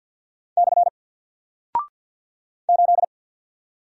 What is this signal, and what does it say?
Morse code, heard as a keyed tone, about 50 wpm, 700 Hz. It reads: / 8
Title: Single Character (Letter:Number:Punctuation:Prosign) Alt 50wpm